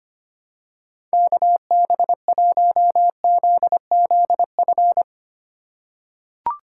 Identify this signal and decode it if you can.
Morse code, heard as a keyed tone, about 25 wpm, 700 Hz. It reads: KB1ZZF
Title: USA Call Signs - 3x Character Spacing - 25wpm